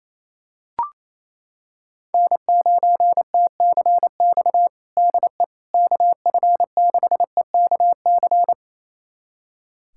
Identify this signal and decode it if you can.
Morse code, heard as a keyed tone, about 28 wpm, 700 Hz. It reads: N9TCX DE KF6EKC
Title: USA Callsign DE Callsign 28wpm